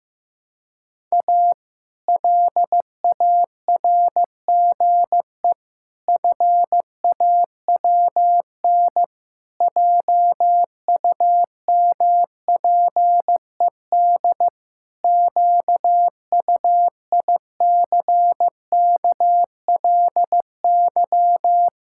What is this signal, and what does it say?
Morse code, heard as a keyed tone, about 14 wpm, 700 Hz. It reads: A LARGE FAWN JUMPED QUICKLY
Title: Pangram Challenge v2 - 15wpm